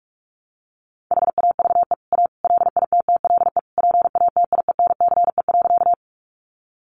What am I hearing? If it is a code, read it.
Morse code, heard as a keyed tone, about 44 wpm, 700 Hz. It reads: HAVE A LITTLE PATIENCE.
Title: Sentences from Top 1300 Words 45wpm